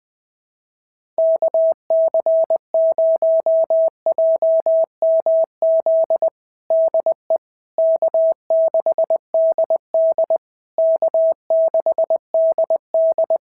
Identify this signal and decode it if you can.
Morse code, heard as a keyed tone, about 19 wpm, 650 Hz. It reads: KC0JMZ DE K6DD K6DD
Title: SOTA QSO Practice 20wpm